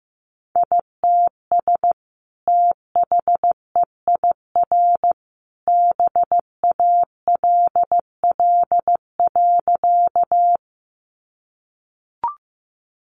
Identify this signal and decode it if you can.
Morse code, heard as a keyed tone, about 15 wpm, 700 Hz. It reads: ITS THEIR BALL.
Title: Sentences from Top 900 Words 15wpm